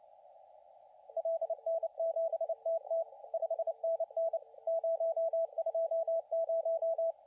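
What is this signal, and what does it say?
Morse code, heard as a keyed tone, about 29 wpm, 660 Hz, with radio noise. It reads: LN7TT 5NN 020
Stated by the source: CW contest audio recorded at LA4C during the SAC CW contest.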